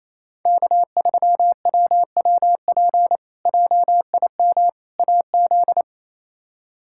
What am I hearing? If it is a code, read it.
Morse code, heard as a keyed tone, about 28 wpm, 700 Hz. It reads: K3WWP JIM AZ
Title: K1USN SST QSO Practice - 28wpm